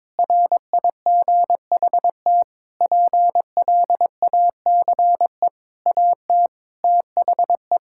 Morse code, heard as a keyed tone, about 22 wpm, 700 Hz. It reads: RIGHT PLACE AT THE
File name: Sets of 9 Words - 22wpm